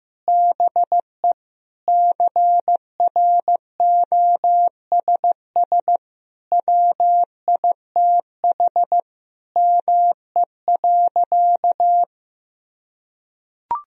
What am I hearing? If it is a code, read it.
Morse code, heard as a keyed tone, about 15 wpm, 700 Hz. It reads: BE CROSS WITH ME.
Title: Sentences from Top 1000 Words - Review Part 1 - 15wpm